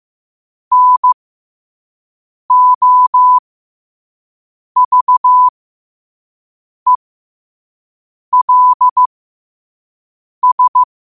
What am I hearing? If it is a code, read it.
Morse code, heard as a keyed tone, about 15 wpm, spaced as for 5 wpm, 1000 Hz. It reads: NOVELS
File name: qotdfiles / 5wpm